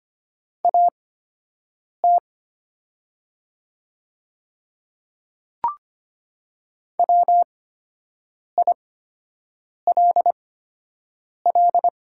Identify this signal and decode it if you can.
Morse code, heard as a keyed tone, about 25 wpm, spaced as for 7 wpm, 700 Hz. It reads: AT WILL
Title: 8x Character Spacing - Top 100 Words - 25wpm